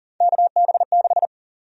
Morse code, heard as a keyed tone, about 40 wpm, 700 Hz. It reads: KB6
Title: USA Call Sign Prefixes - 40wpm